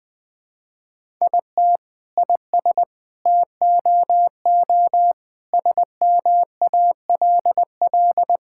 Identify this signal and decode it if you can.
Morse code, heard as a keyed tone, about 20 wpm, 700 Hz. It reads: IT IS TOO SMALL
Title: Sentences from Top 1000 Words - Review Part 6 - 20wpm